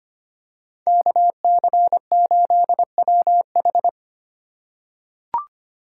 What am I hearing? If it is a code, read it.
Morse code, heard as a keyed tone, about 25 wpm, 700 Hz. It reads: KC8WH
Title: K1USN SST QSO Practice - 25wpm